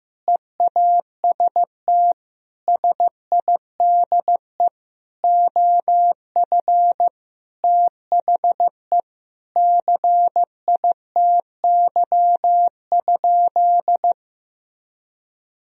Morse code, heard as a keyed tone, about 15 wpm, 700 Hz. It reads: EAST SIDE OF THE CITY?
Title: Sentences from Top 1000 Words - Review Part 3 - 15wpm